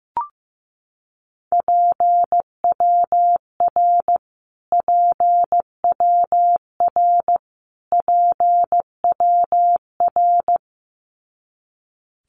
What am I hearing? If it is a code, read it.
Morse code, heard as a keyed tone, about 15 wpm, 700 Hz. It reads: PWR PWR PWR
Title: Abbreviations Repeated 3x - 15wpm